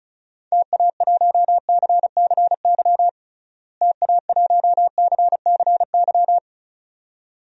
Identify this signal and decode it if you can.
Morse code, heard as a keyed tone, about 36 wpm, 700 Hz. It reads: TA1CCY TA1CCY
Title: DXCC-100 Call Signs - 6 Characters - Repeated 2x - 35wpm